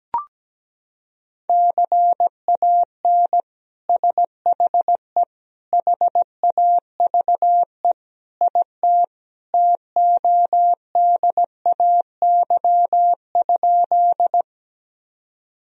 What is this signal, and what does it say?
Morse code, heard as a keyed tone, about 17 wpm, 700 Hz. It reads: CAN SHE HAVE IT TODAY?
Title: Sentences from Top 1000 Words - Review Part 4 - 17wpm